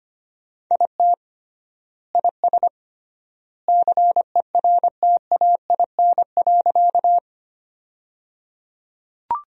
Morse code, heard as a keyed tone, about 25 wpm, 700 Hz. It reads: IT IS CERTAIN.
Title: 3x Word-Spacing - Sentences from Top 500 Words - 25wpm